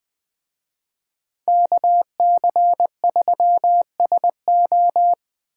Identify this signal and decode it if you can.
Morse code, heard as a keyed tone, about 21 wpm, 700 Hz. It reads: KC3SO